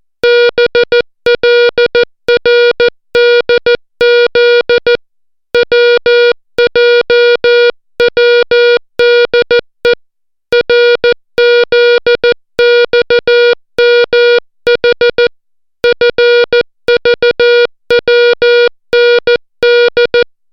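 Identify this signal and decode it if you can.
Morse code, heard as a keyed tone, about 14 wpm, 480 Hz. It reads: BLRDZ WJWDE RZXMH FVWND
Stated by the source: Morse alphabet lesson, 60 characters per minute.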